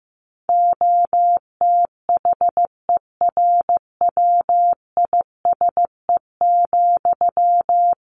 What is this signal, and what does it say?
Morse code, heard as a keyed tone, about 15 wpm, 700 Hz. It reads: OTHERWISE,
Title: Sentences from Top 1500 Words 15wpm